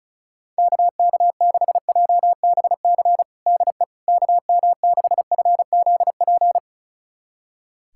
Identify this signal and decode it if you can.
Morse code, heard as a keyed tone, about 34 wpm, 700 Hz. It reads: KK6JBC DE KM6FZP
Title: USA Callsign DE Callsign - 6 Characters - 35wpm